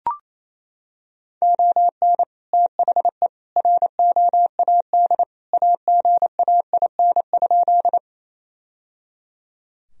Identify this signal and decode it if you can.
Morse code, heard as a keyed tone, about 28 wpm, 700 Hz. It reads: ON THE ROAD AGAIN?
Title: Sentences from Top 1000 Words - Review Part 2 - 28wpm